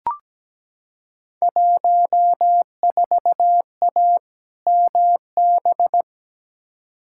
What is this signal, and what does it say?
Morse code, heard as a keyed tone, about 17 wpm, 700 Hz. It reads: 14A MB
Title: Field Day Exchanges Encore - 17wpm